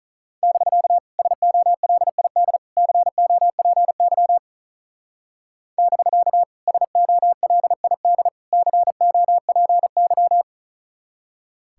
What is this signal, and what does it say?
Morse code, heard as a keyed tone, about 43 wpm, 700 Hz. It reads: <BK> SOLID COPY <BK> SOLID COPY
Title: Speed-Racing Common QSO Phrases - 35wpm